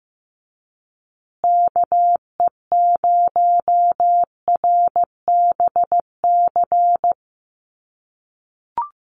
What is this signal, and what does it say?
Morse code, heard as a keyed tone, about 15 wpm, 700 Hz. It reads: KE0RBC
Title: USA Call Signs Repeated 2x 15wpm